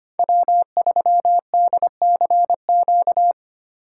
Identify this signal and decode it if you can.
Morse code, heard as a keyed tone, about 29 wpm, 700 Hz. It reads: W3DCQ
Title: Speed Racing - USA 1x3 Call Signs 25wpm